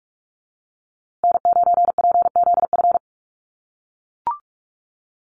Morse code, heard as a keyed tone, about 44 wpm, 700 Hz. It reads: N9PZF
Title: USA Call Signs Repeated 3x 45wpm